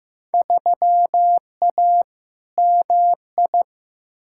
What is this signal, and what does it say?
Morse code, heard as a keyed tone, about 15 wpm, 700 Hz. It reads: 3A MI
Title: Field Day Exchanges - 15wpm